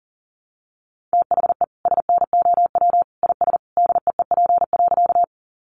Morse code, heard as a keyed tone, about 40 wpm, 700 Hz. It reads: THE SNOW IS DEEP.
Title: Sentences from Top 500 Words 40wpm